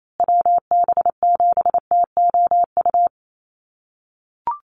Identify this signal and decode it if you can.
Morse code, heard as a keyed tone, about 28 wpm, 700 Hz. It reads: WB7TOU
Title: USA Call Signs Repeated 2x 28wpm